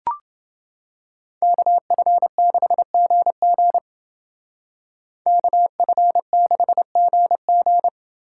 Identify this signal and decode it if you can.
Morse code, heard as a keyed tone, about 31 wpm, 700 Hz. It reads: KF6GG KF6GG